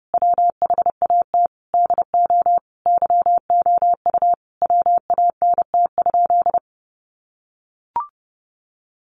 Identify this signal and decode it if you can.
Morse code, heard as a keyed tone, about 30 wpm, 700 Hz. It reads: WHAT DO YOU WANT?
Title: Sentences from Top 1100 Words 30wpm